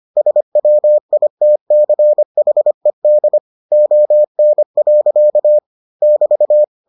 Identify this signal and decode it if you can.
Morse code, heard as a keyed tone, about 25 wpm, 600 Hz. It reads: SWITCHED ON. <BT>